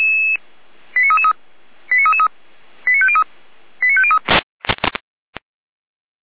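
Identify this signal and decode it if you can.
Five-tone selective call (ZVEI), heard as quick runs of five tones, tones of about 70 ms. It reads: 89333 89333 89503 89503
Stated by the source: ZVEI five-tone selective call received with an SDR and recorded by Feuerwehr Orlamünde.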